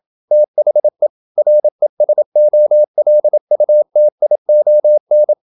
Morse code, heard as a keyed tone, about 27 wpm, 600 Hz. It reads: THE RESOLUTION
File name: Never 0027wpm